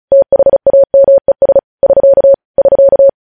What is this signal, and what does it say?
Morse code, heard as a keyed tone, about 36 wpm, 550 Hz. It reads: THAMES <SK> <SK>
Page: news headlines in morse code at 35 wpm